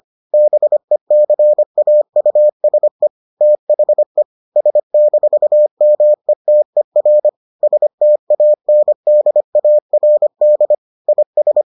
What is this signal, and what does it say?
Morse code, heard as a keyed tone, about 25 wpm, 600 Hz. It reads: BECAUSE THE S-METER STANDARD IS